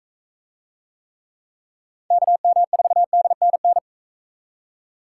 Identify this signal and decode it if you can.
Morse code, heard as a keyed tone, about 44 wpm, 700 Hz. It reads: KM4DNN